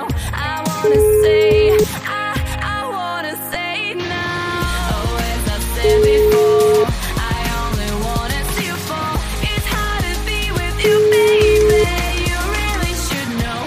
Ringback tone (Europe, 425 Hz), with noise.